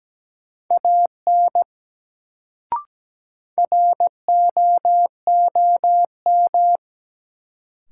Morse code, heard as a keyed tone, about 17 wpm, 700 Hz. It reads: AN ROOM
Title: Sentences from Top 300 Words - New Word Drill - 17wpm